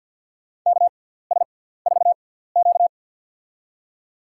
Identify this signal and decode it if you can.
Morse code, heard as a keyed tone, about 46 wpm, spaced as for 17 wpm, 690 Hz. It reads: KS4Q
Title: USA Call Signs - 6x Character Spacing - 50wpm